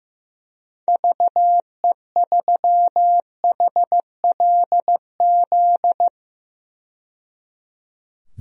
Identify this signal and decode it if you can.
Morse code, heard as a keyed tone, about 15 wpm, 700 Hz. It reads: VE3HLZ
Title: DXCC-100 Call Signs - 6 Characters - 15wpm